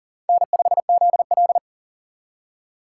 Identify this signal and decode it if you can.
Morse code, heard as a keyed tone, about 51 wpm, 700 Hz. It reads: N5ZL